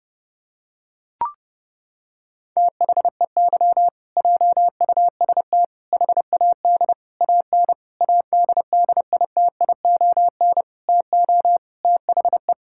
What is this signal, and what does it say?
Morse code, heard as a keyed tone, about 28 wpm, 700 Hz. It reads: THEY JUST HAD AN ADDITION TO THE
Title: Sentences from Top 1000 Words - Review Part 6 - 30wpm